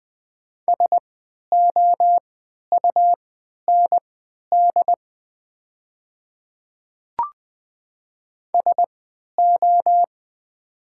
Morse code, heard as a keyed tone, about 20 wpm, spaced as for 11 wpm, 700 Hz. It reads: SOUND SO
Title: 3x Character Spacing - Top 100 Words - 20wpm